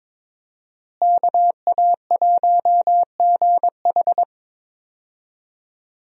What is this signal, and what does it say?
Morse code, heard as a keyed tone, about 22 wpm, 700 Hz. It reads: KA1GH
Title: USA Call Signs - 5 Characters - Repeated 3x - 22wpm